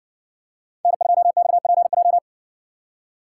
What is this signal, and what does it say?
Morse code, heard as a keyed tone, about 60 wpm, 700 Hz. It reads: N2XPJ